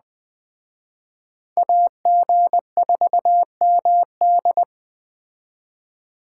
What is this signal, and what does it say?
Morse code, heard as a keyed tone, about 20 wpm, 700 Hz. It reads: AG4MD
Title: Speed Racing - USA 2x2 Call Signs 17wpm